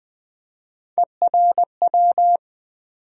Morse code, heard as a keyed tone, about 20 wpm, 700 Hz. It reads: ERW
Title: USA Call Sign Postfixes - 3 Letters 20wpm